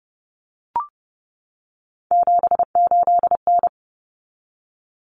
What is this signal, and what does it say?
Morse code, heard as a keyed tone, about 30 wpm, 700 Hz. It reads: 78N
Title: Common Cut Numbers - 3-Digit 30wpm